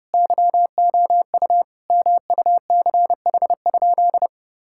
Morse code, heard as a keyed tone, about 30 wpm, 700 Hz. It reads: YOU MUCH?
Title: Sentences from Top 1000 Words - Review Part 6 - 30wpm